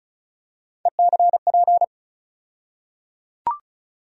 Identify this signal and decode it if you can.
Morse code, heard as a keyed tone, about 36 wpm, 700 Hz. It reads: ECP
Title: USA Call Sign Postfixes - 3 Letters 35wpm